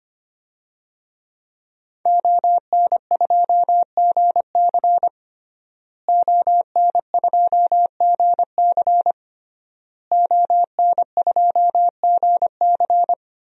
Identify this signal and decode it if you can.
Morse code, heard as a keyed tone, about 25 wpm, 700 Hz. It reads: ON2GC ON2GC ON2GC
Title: DXCC-100 Call Signs - 5 Characters - Repeated 3x - 25wpm